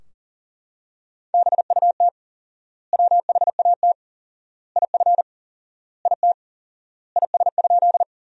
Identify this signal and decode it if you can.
Morse code, heard as a keyed tone, about 40 wpm, 700 Hz. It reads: BUT WHAT IF IT IS?
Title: 4x Word-Spacing - Sentences from Top 100 Words - 40wpm